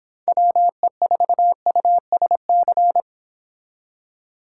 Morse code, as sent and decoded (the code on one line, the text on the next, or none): .-- . ....- ..- ... -.-.
WE4USC